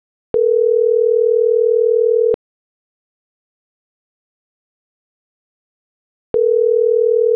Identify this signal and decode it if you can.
Ringback tone (North America), 440 + 480 Hz.